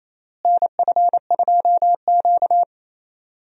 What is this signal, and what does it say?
Morse code, heard as a keyed tone, about 28 wpm, 700 Hz. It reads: NF2Q
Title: USA Call Signs - 4 Characters - Repeated 3x - 28wpm